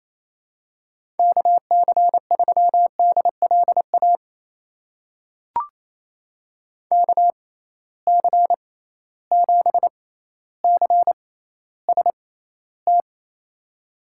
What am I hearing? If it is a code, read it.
Morse code, heard as a keyed tone, about 28 wpm, spaced as for 14 wpm, 700 Hz. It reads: KC3DLA KC7CST
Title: USA Call Signs - 6 Characters - 6x Character Spacing - 28wpm